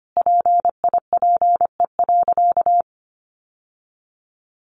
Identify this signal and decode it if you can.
Morse code, heard as a keyed tone, about 25 wpm, 700 Hz. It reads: PIPE.
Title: Sentences from Top 1500 Words 25wpm